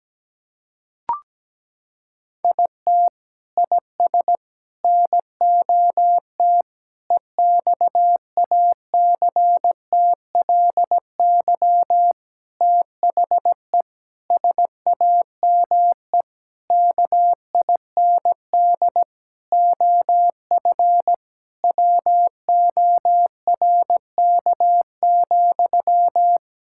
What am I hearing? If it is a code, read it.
Morse code, heard as a keyed tone, about 17 wpm, 700 Hz. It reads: IT IS NOT EXACTLY THE SAME KIND OF WORK,
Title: Sentences from Top 1000 Words - Review Part 6 - 17wpm